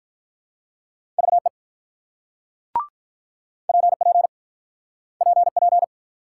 Morse code, heard as a keyed tone, about 48 wpm, 700 Hz. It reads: VE PP PP